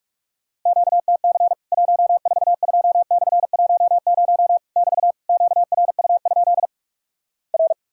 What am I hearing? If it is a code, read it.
Morse code, heard as a keyed tone, about 44 wpm, 700 Hz. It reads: QTC 142/10 <BT> QRU? R
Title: Worked All Europe DX Contest - QTC Exchange - 45wpm